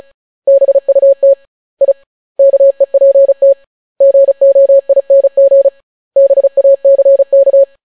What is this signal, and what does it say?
Morse code, heard as a keyed tone, about 36 wpm, 550 Hz. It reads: BUT I KEPT GOING BACK